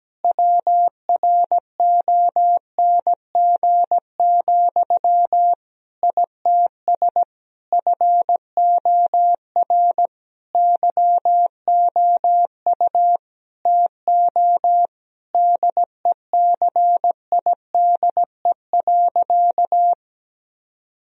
Morse code, heard as a keyed tone, about 17 wpm, 700 Hz. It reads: WRONG, ITS FOR YOU TO DECIDE.
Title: Sentences from Top 1000 Words - Review Part 3 - 17wpm